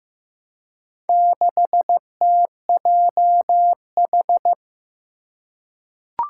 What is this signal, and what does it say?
Morse code, heard as a keyed tone, about 15 wpm, 700 Hz. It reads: 6TJH
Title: Four Challenging Letter-Number Combinations 15wpm